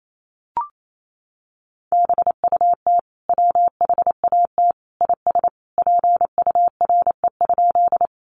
Morse code, heard as a keyed tone, about 28 wpm, 700 Hz. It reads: BUT WHAT IS PURE?
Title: Sentences from Top 1500 Words 28wpm